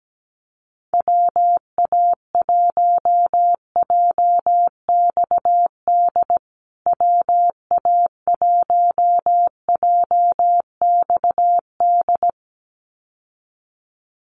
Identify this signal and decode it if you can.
Morse code, heard as a keyed tone, about 17 wpm, 700 Hz. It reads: WA1JXD WA1JXD
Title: USA Call Signs Repeated 2x 17wpm